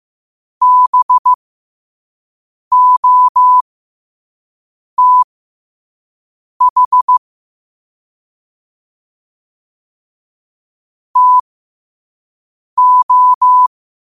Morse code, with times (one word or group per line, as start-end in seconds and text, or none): 0.6-7.2 BOTH
11.2-13.7 TO